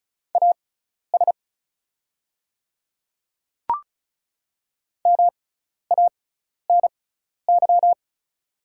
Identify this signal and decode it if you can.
Morse code, heard as a keyed tone, about 34 wpm, spaced as for 12 wpm, 690 Hz. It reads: AS MANY
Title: 6x Character Spacing - Top 500 Words - 35wpm